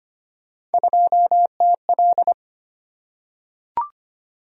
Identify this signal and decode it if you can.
Morse code, heard as a keyed tone, about 25 wpm, 700 Hz. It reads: 2TL
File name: Three Letter-Number Combinations - 25wpm